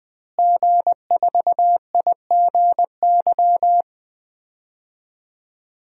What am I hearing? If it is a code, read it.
Morse code, heard as a keyed tone, about 20 wpm, 700 Hz. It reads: G4IGY
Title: DXCC-100 Call Signs - 5 Characters - Repeated 2x - 20wpm